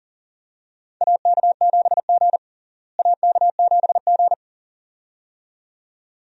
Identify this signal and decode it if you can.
Morse code, heard as a keyed tone, about 40 wpm, 700 Hz. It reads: AK7G AK7G
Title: USA Call Signs - 4 Characters - Repeated 2x - 40wpm